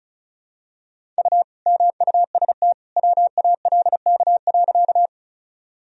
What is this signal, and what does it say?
Morse code, heard as a keyed tone, about 36 wpm, 700 Hz. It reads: U MUST WALK.
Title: Sentences from Top 300 Words and Abbreviations 35wpm